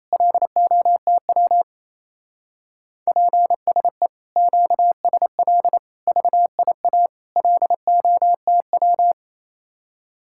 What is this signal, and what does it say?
Morse code, heard as a keyed tone, about 34 wpm, 700 Hz. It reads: LOTW PSE QSL VIA LOTW
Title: Speed-Racing Common QSO Phrases - 28wpm